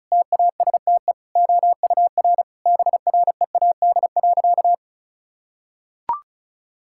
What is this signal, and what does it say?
Morse code, heard as a keyed tone, about 36 wpm, 700 Hz. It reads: TASTE OUR BREAD.
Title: Sentences from Top 1600 Words 35wpm